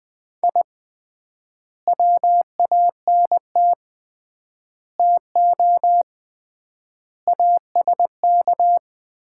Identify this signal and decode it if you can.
Morse code, heard as a keyed tone, about 21 wpm, 690 Hz. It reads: I WANT TO ASK
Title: 3x Word-Spacing - Sets of 5 Words - 20wpm